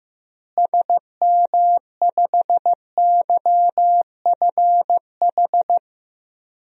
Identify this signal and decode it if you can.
Morse code, heard as a keyed tone, about 16 wpm, 700 Hz. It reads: SM5YFH
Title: DXCC-100 Call Signs - 6 Characters - Repeated 2x - 15wpm